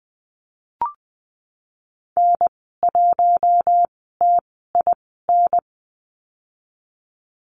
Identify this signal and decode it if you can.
Morse code, heard as a keyed tone, about 20 wpm, spaced as for 14 wpm, 700 Hz. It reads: N1TIN
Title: 042 Lesson 8g - Number 1 - Call Signs 20wpm